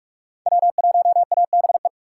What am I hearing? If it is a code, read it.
Morse code, heard as a keyed tone, about 46 wpm, 700 Hz. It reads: W1ABE
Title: USA Call Signs - 5 Characters - 3x Character Spacing - 45wpm